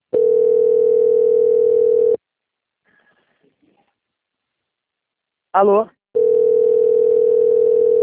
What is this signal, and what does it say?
Ringback tone (North America), 440 + 480 Hz, with speech.
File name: out- - - - - .3449